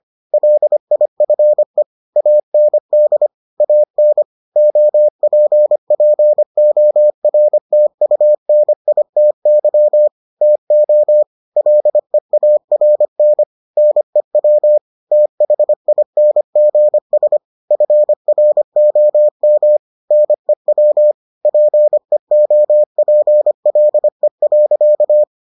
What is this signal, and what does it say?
Morse code, heard as a keyed tone, about 25 wpm, 600 Hz. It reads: LIFE AND AN OPPORTUNITY TO LEARN NEW THINGS FROM NEW PEOPLE.